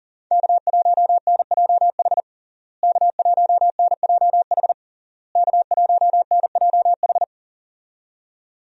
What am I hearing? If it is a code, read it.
Morse code, heard as a keyed tone, about 40 wpm, 700 Hz. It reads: K1NJH K1NJH K1NJH
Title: USA Call Signs - 5 Characters - Repeated 3x - 40wpm